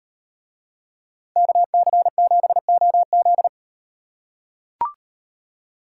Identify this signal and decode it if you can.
Morse code, heard as a keyed tone, about 35 wpm, 700 Hz. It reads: KC7OZ